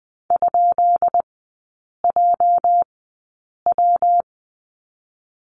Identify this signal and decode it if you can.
Morse code, heard as a keyed tone, about 20 wpm, spaced as for 8 wpm, 700 Hz. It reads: ?JW
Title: 205 Lesson 33e - Letter j - Three Characters 20wpm